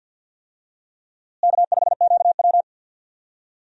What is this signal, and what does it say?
Morse code, heard as a keyed tone, about 47 wpm, 700 Hz. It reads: K5QW
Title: USA Call Signs - 4 Characters - Repeated 3x - 50wpm